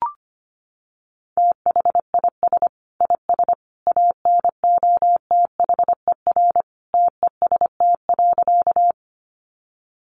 Morse code, heard as a keyed tone, about 25 wpm, 700 Hz. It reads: THIS IS ANOTHER TEST.
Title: Sentences from Top 600 Words 25wpm